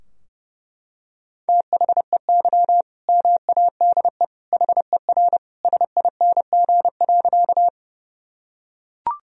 Morse code, heard as a keyed tone, about 30 wpm, 700 Hz. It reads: THEY MADE HER SING.
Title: Sentences from Top 1000 Words - Review Part 2 - 30wpm